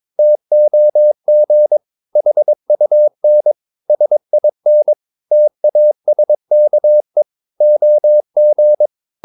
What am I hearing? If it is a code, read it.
Morse code, heard as a keyed tone, about 22 wpm, 600 Hz. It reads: TOG HUN SIN TASKE OG
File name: Ferie QRS 22wpm